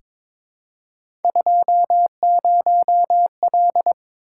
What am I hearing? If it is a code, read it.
Morse code, heard as a keyed tone, about 22 wpm, 700 Hz. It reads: 20L
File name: practice / 22wpm / Three Letter-Number Combinations - 22wpm